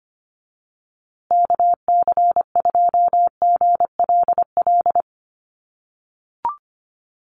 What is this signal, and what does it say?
Morse code, heard as a keyed tone, about 25 wpm, 700 Hz. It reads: KC2GLL
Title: USA Call Signs Repeated 3x 25wpm